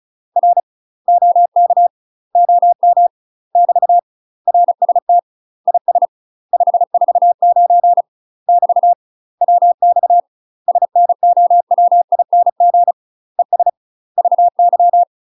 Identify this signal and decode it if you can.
Morse code, heard as a keyed tone, about 34 wpm, 700 Hz. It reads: R OK OM <BT> RST IS 549 <BT> WX SNOWING ES VY
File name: UK MASTER QSOs TRAINING / 35WPM / UK MASTER QSOs 0008